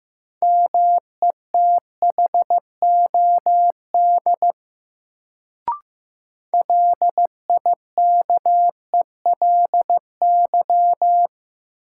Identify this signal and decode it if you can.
Morse code, heard as a keyed tone, about 15 wpm, 700 Hz. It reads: METHOD LIKELY
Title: Sentences from Top 600 Words - New Word Drill - 15wpm